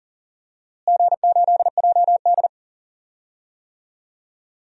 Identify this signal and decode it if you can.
Morse code, heard as a keyed tone, about 40 wpm, 700 Hz. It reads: G8JD
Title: DXCC-100 Call Signs - 4 Characters - Repeated 2x - 40wpm